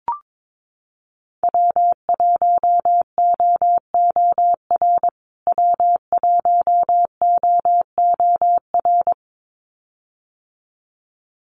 Morse code, heard as a keyed tone, about 22 wpm, 700 Hz. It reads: W1OOR W1OOR
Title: USA Call Signs Repeated 2x 22wpm